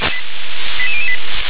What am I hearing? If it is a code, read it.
Five-tone selective call (ZVEI), heard as quick runs of five tones, tones of about 70 ms. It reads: 88008